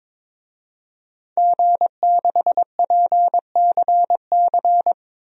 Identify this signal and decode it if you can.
Morse code, heard as a keyed tone, about 22 wpm, 700 Hz. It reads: G6PCC
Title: DXCC-100 Call Signs - 5 Characters - 22wpm